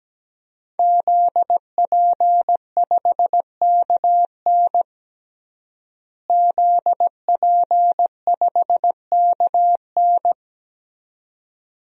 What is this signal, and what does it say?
Morse code, heard as a keyed tone, about 17 wpm, 700 Hz. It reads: ZP5KN ZP5KN
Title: DXCC-100 Call Signs - 5 Characters - Repeated 2x - 17wpm